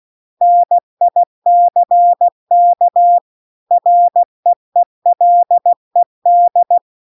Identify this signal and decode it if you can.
Morse code, heard as a keyed tone, about 16 wpm, 700 Hz. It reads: NICK REELED